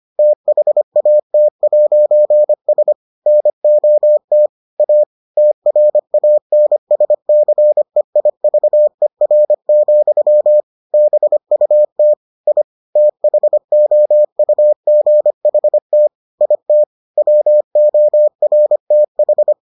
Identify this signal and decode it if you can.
Morse code, heard as a keyed tone, about 25 wpm, 600 Hz. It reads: THAT'S NOT A TRANSCEIVER, BUT I THOUGHT IT WORTH